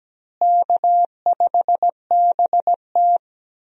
Morse code, heard as a keyed tone, about 17 wpm, 700 Hz. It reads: K5BT
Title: USA Call Signs - 4 Characters - Repeated 3x - 17wpm